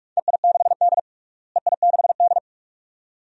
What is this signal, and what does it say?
Morse code, heard as a keyed tone, about 42 wpm, 700 Hz. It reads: EI6D EI6D
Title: DXCC-100 Call Signs - 4 Characters - Repeated 2x - 45wpm